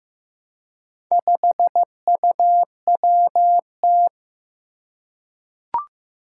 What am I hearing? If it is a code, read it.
Morse code, heard as a keyed tone, about 15 wpm, 700 Hz. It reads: 5UWT
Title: Four Challenging Letter-Number Combinations 15wpm